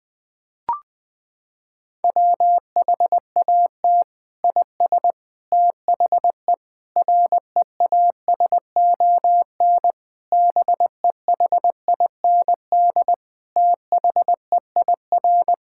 Morse code, heard as a keyed tone, about 20 wpm, 700 Hz. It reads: WHAT IS THE REASON BEHIND THEIR
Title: Sentences from Top 1000 Words - Review Part 5 - 20wpm